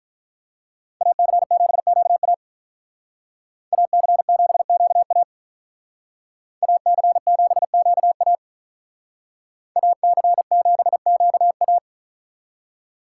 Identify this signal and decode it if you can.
Morse code, heard as a keyed tone, about 45 wpm, 700 Hz. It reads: AC7QA AC7QA AC7QA AC7QA